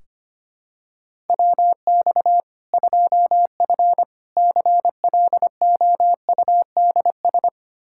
Morse code, heard as a keyed tone, about 25 wpm, 700 Hz. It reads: WX 2F CLOUDS
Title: Practice QSO - Exchange 2 - Info - 25wpm